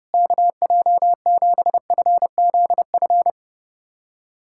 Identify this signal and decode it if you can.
Morse code, heard as a keyed tone, about 30 wpm, 700 Hz. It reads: KJ7FZF
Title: USA Callsign DE Callsign 30wpm